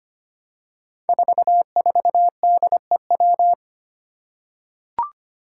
Morse code, heard as a keyed tone, about 25 wpm, 700 Hz. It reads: 44DEW
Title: Five Challenging Letter-Number Combinations 25wpm